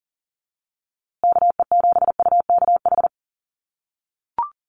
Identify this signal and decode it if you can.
Morse code, heard as a keyed tone, about 40 wpm, 700 Hz. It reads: KE7UKH
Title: USA Call Signs Repeated 3x 40wpm